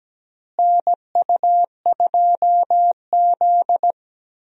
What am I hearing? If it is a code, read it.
Morse code, heard as a keyed tone, about 17 wpm, 700 Hz. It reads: NU2Z
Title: USA Call Signs - 4 Characters - Repeated 3x - 17wpm